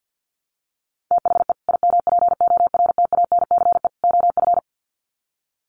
Morse code, heard as a keyed tone, about 50 wpm, 700 Hz. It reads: THE IMPORTANCE OF
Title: Sets of 3 Words 50wpm